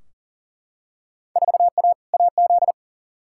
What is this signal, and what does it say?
Morse code, heard as a keyed tone, about 40 wpm, 700 Hz. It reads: 4A AZ
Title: Field Day Exchanges Encore - 40wpm